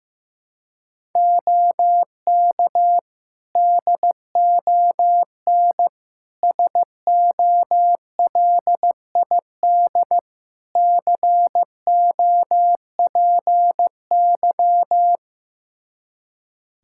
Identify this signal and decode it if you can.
Morse code, heard as a keyed tone, about 16 wpm, 700 Hz. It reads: OK DON SOLID COPY
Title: Practice QSO - Exchange 3 - More Info - 15wpm